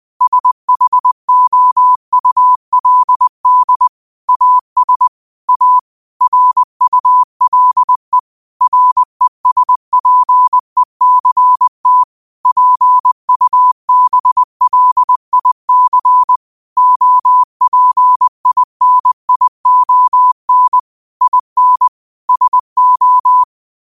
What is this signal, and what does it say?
Morse code, heard as a keyed tone, about 20 wpm, 1000 Hz. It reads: SHOULD AS A RULE RESPECT PUBLIC OPINION IN SO
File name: qotdfiles / 20wpm